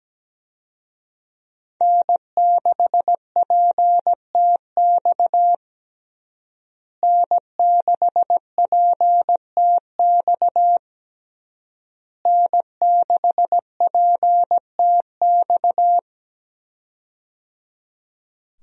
Morse code, heard as a keyed tone, about 17 wpm, 700 Hz. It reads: N6PTX N6PTX N6PTX
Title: USA Call Signs - 5 Characters - Repeated 3x - 17wpm